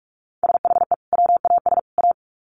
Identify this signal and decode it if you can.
Morse code, heard as a keyed tone, about 44 wpm, 700 Hz. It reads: SHE WAS A
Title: Sets of 3 Words 45wpm